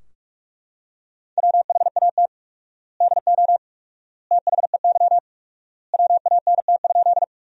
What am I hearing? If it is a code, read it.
Morse code, heard as a keyed tone, about 41 wpm, 700 Hz. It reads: WHAT DO THEY WANT?
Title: 4x Word-Spacing - Sentences from Top 200 Words - 45wpm